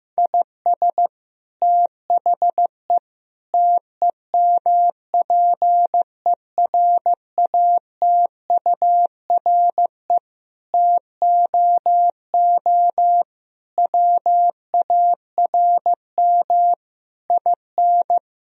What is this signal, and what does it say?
Morse code, heard as a keyed tone, about 15 wpm, 700 Hz. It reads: IS THE TEMPERATURE TOO WARM IN
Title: Sentences from Top 1000 Words - Review Part 4 - 15wpm